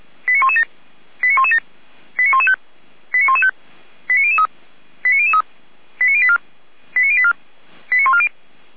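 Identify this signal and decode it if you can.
Five-tone selective call (ZVEI), heard as quick runs of five tones, tones of about 70 ms. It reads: 89117 89117 89115 89115 89003 89003 89084 89084 89140
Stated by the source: ZVEI five-tone selective call received on an SDR.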